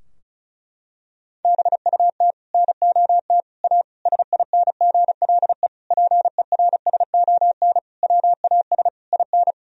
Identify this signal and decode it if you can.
Morse code, heard as a keyed tone, about 34 wpm, 700 Hz. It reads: BUT NOT A SINGLE PERSON WAS IN
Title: Sentences from Top 1000 Words - Review Part 4 - 35wpm